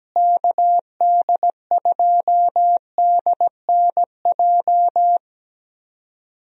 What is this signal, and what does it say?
Morse code, heard as a keyed tone, about 17 wpm, 700 Hz. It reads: KD2DNJ
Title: USA Call Signs - 6 Characters - Repeated 2x - 17wpm